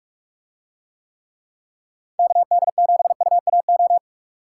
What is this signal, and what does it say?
Morse code, heard as a keyed tone, about 45 wpm, 700 Hz. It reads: KD7UAO